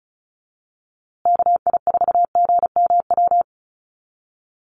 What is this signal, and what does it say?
Morse code, heard as a keyed tone, about 34 wpm, 700 Hz. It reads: KI4GMW
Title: USA Call Signs Repeated 3x 35wpm